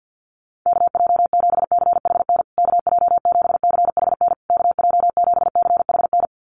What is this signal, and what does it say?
Morse code, heard as a keyed tone, about 50 wpm, 700 Hz. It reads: KJ7CHN KJ7CHN KJ7CHN
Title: USA Call Signs Repeated 3x 50wpm